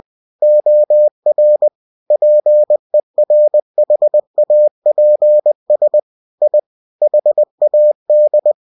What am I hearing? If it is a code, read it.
Morse code, heard as a keyed tone, about 20 wpm, 600 Hz. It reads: OR PERHAPS I HAD
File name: cw-books / winnie-the-pooh / winnie-20wpm-page-0107